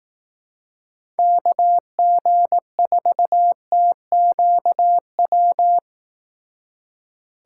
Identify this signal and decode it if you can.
Morse code, heard as a keyed tone, about 18 wpm, 700 Hz. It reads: KG4TQW
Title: Speed Racing - USA 2x3 Call Signs 15wpm